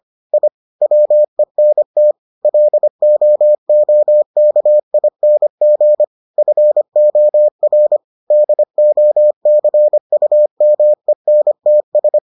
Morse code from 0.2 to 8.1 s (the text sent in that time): I WENT LOOKING FOR